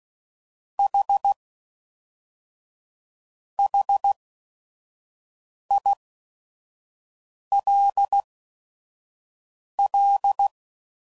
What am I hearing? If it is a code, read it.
Morse code, heard as a keyed tone, about 16 wpm, spaced as for 5 wpm, 800 Hz. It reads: H HILL
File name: CW Code Groups @5WPM / QSOs / kc7sac